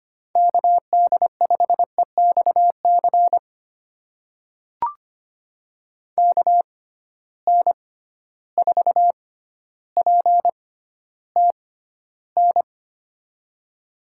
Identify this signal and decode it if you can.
Morse code, heard as a keyed tone, about 25 wpm, spaced as for 13 wpm, 700 Hz. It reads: KD5EXC KN4PTN
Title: USA Call Signs - 6 Characters - 6x Character Spacing - 25wpm